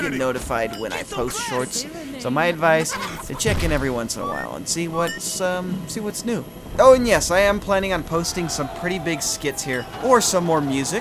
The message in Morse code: none

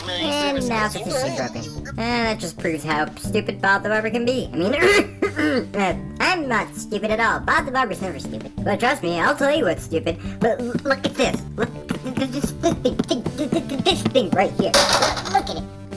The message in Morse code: none

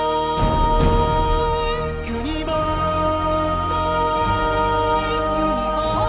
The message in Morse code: none

